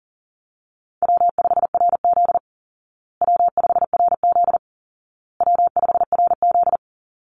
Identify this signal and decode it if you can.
Morse code, heard as a keyed tone, about 40 wpm, 700 Hz. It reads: W5RZ W5RZ W5RZ
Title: Easy Call Signs Repeated 3x - 40wpm